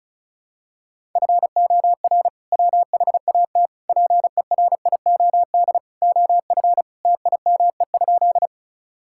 Morse code, heard as a keyed tone, about 34 wpm, 700 Hz. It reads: FOR WHAT PERIOD OF TIME?
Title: Sentences from Top 1000 Words - Review Part 5 - 35wpm